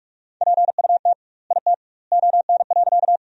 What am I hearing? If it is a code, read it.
Morse code, heard as a keyed tone, about 46 wpm, 700 Hz. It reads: PUT IT ON.
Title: 2x Word-Spacing - Sentences from Top 200 Words - 45wpm